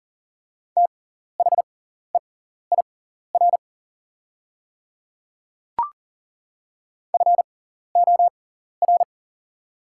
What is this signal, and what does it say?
Morse code, heard as a keyed tone, about 40 wpm, spaced as for 14 wpm, 700 Hz. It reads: THEIR FOR
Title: 6x Character Spacing - Top 100 Words - 40wpm